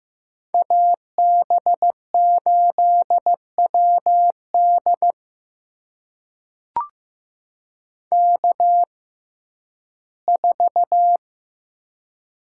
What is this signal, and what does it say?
Morse code, heard as a keyed tone, about 15 wpm, spaced as for 11 wpm, 700 Hz. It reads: AB8WD K4
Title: USA Call Signs - 5 Characters - 6x Character Spacing - 15wpm